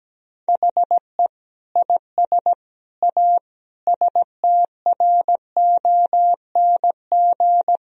Morse code, heard as a keyed tone, about 17 wpm, 700 Hz. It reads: HE IS A STRONG
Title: Sentences from Top 1000 Words - Review Part 3 - 17wpm